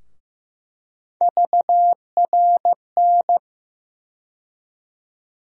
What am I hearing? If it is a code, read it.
Morse code, heard as a keyed tone, about 15 wpm, 700 Hz. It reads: VRN
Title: USA Call Sign Postfixes - 3 Letters 15wpm